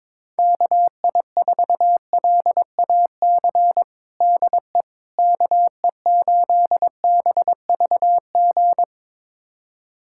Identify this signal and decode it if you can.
Morse code, heard as a keyed tone, about 22 wpm, 700 Hz. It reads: KI4LAC DE KE8BVG
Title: USA Callsign DE Callsign - 6 Characters - 22wpm